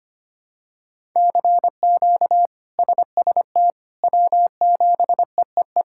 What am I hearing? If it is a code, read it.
Morse code, heard as a keyed tone, about 25 wpm, 700 Hz. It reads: CQ SST W7EEE
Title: K1USN SST QSO Practice - 25wpm